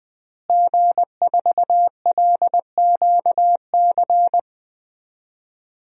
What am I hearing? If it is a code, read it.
Morse code, heard as a keyed tone, about 20 wpm, 700 Hz. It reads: G4LQC
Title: DXCC-100 Call Signs - 5 Characters - Repeated 2x - 20wpm